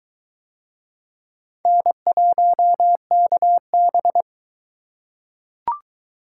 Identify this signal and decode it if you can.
Morse code, heard as a keyed tone, about 22 wpm, 700 Hz. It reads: N1KB